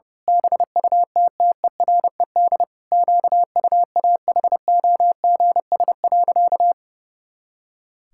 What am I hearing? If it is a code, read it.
Morse code, heard as a keyed tone, about 30 wpm, 700 Hz. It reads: BUTTERED QUAHOGS.
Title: Pangram Challenge v2 - 30wpm